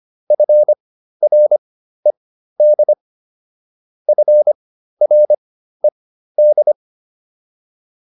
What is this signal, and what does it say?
Morse code, heard as a keyed tone, about 25 wpm, spaced as for 13 wpm, 600 Hz. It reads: FRED FRED